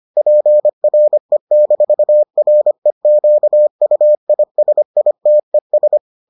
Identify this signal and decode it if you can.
Morse code, heard as a keyed tone, about 25 wpm, 600 Hz. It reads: PRE-REQUISITES